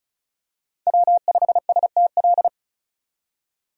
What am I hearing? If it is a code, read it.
Morse code, heard as a keyed tone, about 34 wpm, 700 Hz. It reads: W5STL
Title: USA Call Signs - 5 Characters - 35wpm